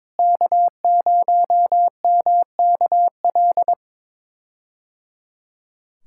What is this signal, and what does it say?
Morse code, heard as a keyed tone, about 22 wpm, 700 Hz. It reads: K0MKL
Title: K1USN SST QSO Practice - 22wpm